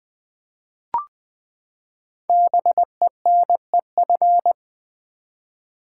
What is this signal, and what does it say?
Morse code, heard as a keyed tone, about 20 wpm, 700 Hz. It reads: BENEF
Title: Build A Word - 5 Letters At A Time - 10-Letter Words - 20wpm